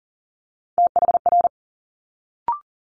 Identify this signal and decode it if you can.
Morse code, heard as a keyed tone, about 40 wpm, 700 Hz. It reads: THR
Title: Multi-letter Phonemes Quick Fire 40wpm